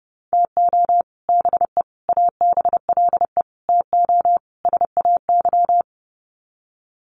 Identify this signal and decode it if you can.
Morse code, heard as a keyed tone, about 30 wpm, 700 Hz. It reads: TO BE ABLE TO SAY
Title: Sets of 5 Words Encore2 30wpm